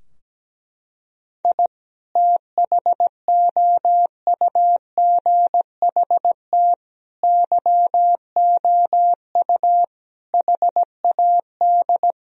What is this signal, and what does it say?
Morse code, heard as a keyed tone, about 17 wpm, 700 Hz. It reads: I THOUGHT YOU HAD
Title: Sentences from Top 1000 Words - Review Part 3 - 17wpm